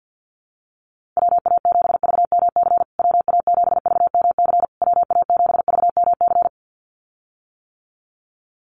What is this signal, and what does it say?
Morse code, heard as a keyed tone, about 48 wpm, 700 Hz. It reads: WA7VMC WA7VMC WA7VMC
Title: USA Call Signs Repeated 3x 50wpm